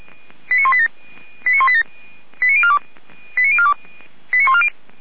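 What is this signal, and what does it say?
Five-tone selective call (ZVEI), heard as quick runs of five tones, tones of about 80 ms. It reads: 89187 89187 89042 89042 89140
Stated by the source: ZVEI five-tone selective call for address 89097, received by SDR.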